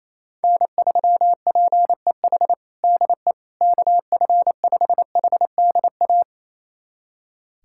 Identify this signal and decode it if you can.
Morse code, heard as a keyed tone, about 28 wpm, 700 Hz. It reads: N3PEH DE KF5HDA
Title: USA Callsign DE Callsign 28wpm